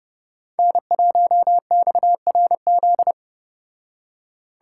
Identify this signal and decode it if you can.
Morse code, heard as a keyed tone, about 33 wpm, 700 Hz. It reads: N1XRZ